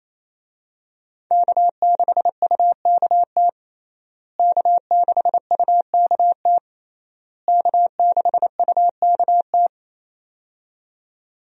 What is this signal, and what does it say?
Morse code, heard as a keyed tone, about 28 wpm, 700 Hz. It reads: K6UKT K6UKT K6UKT
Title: USA Call Signs - 5 Characters - Repeated 3x - 28wpm